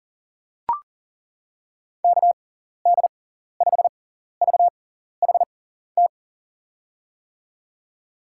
Morse code, heard as a keyed tone, about 40 wpm, spaced as for 14 wpm, 700 Hz. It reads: KD5VHT
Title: USA Call Signs - 6 Characters - 6x Character Spacing - 40wpm